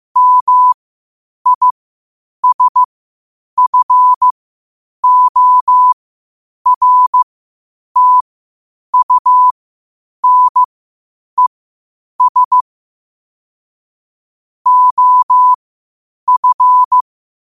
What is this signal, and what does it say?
Morse code, heard as a keyed tone, about 15 wpm, spaced as for 8 wpm, 1000 Hz. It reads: MISFORTUNES OF
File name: qotdfiles / 8wpm